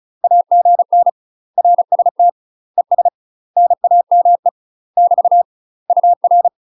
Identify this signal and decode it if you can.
Morse code, heard as a keyed tone, about 34 wpm, 700 Hz. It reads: AGN RST ES NAME <BT> UR
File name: UK MASTER QSOs TRAINING / 35WPM / UK MASTER QSOs 0001